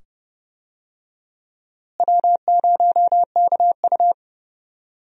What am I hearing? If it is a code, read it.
Morse code, heard as a keyed tone, about 30 wpm, 700 Hz. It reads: W0KU
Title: USA Call Signs - 4 Characters - Repeated 2x - 30wpm